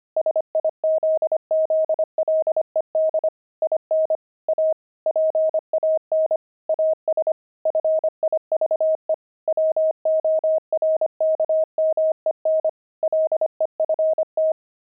Morse code, heard as a keyed tone, about 25 wpm, 630 Hz. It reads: SIZZLED IN A PAN AS FIVE WORKMEN LEFT
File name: pangram-fs-25-cs-25-steaks